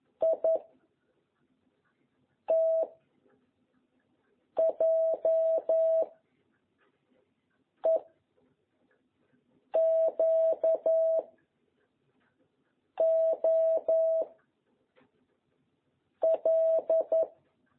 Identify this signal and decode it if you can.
Morse code, heard as a keyed tone, about 11 wpm, 660 Hz. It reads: I T J E Q O L
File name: jh8cbh / 2021cw